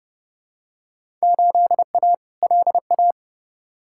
Morse code, heard as a keyed tone, about 30 wpm, 700 Hz. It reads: 8A LA
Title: Field Day Exchanges Encore - 30wpm